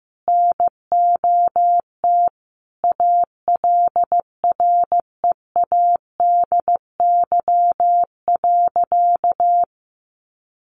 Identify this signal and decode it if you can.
Morse code, heard as a keyed tone, about 15 wpm, 700 Hz. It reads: NOT ALREADY.
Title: Sentences from Top 900 Words 15wpm